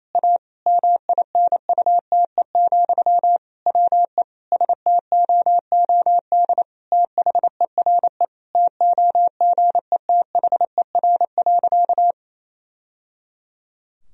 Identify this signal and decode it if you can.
Morse code, heard as a keyed tone, about 28 wpm, 700 Hz. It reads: A MINUTE, WE STOOD THERE TOGETHER.
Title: Sentences from Top 1000 Words - Review Part 2 - 28wpm